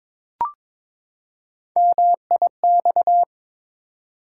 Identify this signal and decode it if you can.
Morse code, heard as a keyed tone, about 22 wpm, 700 Hz. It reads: MIX
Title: Three Letter Words - 22wpm